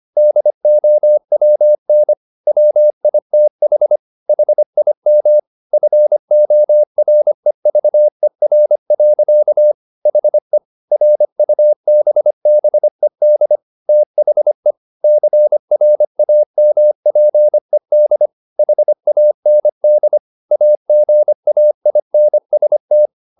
Morse code, heard as a keyed tone, about 25 wpm, 600 Hz. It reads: DOWN WITH HIM FOREVER. HE RUBBED THE CRAMPED HAND AGAINST
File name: the old man and the sea part 1 25wpm